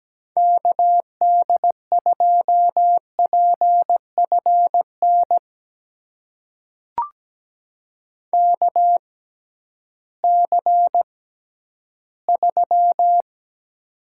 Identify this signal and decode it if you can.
Morse code, heard as a keyed tone, about 17 wpm, spaced as for 11 wpm, 700 Hz. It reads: KD2PFN KC3
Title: USA Call Signs - 6 Characters - 6x Character Spacing - 17wpm